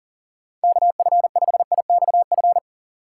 Morse code, heard as a keyed tone, about 37 wpm, 700 Hz. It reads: KF5IXF
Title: USA Call Signs - 6 Characters - 3x Character Spacing - 40wpm